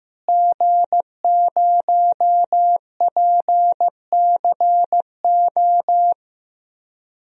Morse code, heard as a keyed tone, about 15 wpm, 700 Hz. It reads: G0PCO
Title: DXCC-100 Call Signs - 5 Characters - Repeated 3x - 15wpm